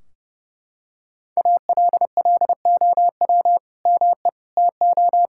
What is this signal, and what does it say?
Morse code, heard as a keyed tone, about 30 wpm, 700 Hz. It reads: ALLOW ME TO
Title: Sentences from Top 1600 Words 30wpm